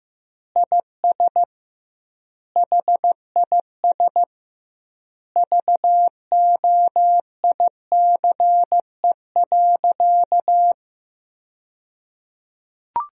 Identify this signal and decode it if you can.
Morse code, heard as a keyed tone, about 15 wpm, 700 Hz. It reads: IS HIS VOICE.
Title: 2x Word-Spacing - Sentences from Top 500 Words - 15wpm